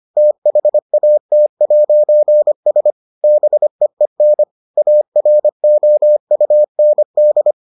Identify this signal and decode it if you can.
Morse code, heard as a keyed tone, about 25 wpm, 600 Hz. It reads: THAT'S BEEN AROUND